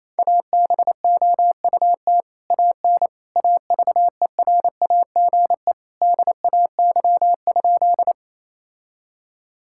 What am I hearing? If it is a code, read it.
Morse code, heard as a keyed tone, about 28 wpm, 700 Hz. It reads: ABOUT AN AVERAGE DAY?
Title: Sentences from Top 1000 Words - Review Part 6 - 28wpm